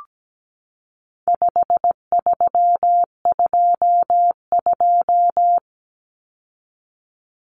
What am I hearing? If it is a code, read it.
Morse code, heard as a keyed tone, about 17 wpm, 700 Hz. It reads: 5322
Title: Common Cut Numbers - 4-Digit 17wpm